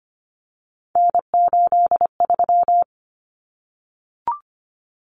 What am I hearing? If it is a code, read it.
Morse code, heard as a keyed tone, about 25 wpm, 700 Hz. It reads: N83
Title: Common Cut Numbers - 3-Digit 25wpm